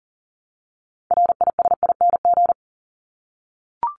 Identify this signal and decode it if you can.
Morse code, heard as a keyed tone, about 40 wpm, 700 Hz. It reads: RISING
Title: ING Suffix - Top 100 Words with Base Words 40wpm